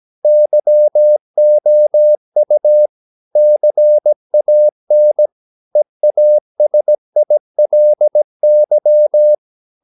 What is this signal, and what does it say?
Morse code, heard as a keyed tone, about 17 wpm, 600 Hz. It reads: YOU CAN EASILY
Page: Romanen Never af Ken Follet på engelsk.